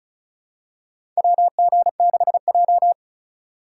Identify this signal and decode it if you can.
Morse code, heard as a keyed tone, about 34 wpm, 700 Hz. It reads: WG6J
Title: USA Call Signs - 4 Characters - 2x Character Spacing - 35wpm